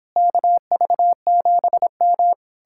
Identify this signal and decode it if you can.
Morse code, heard as a keyed tone, about 29 wpm, 700 Hz. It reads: KV7M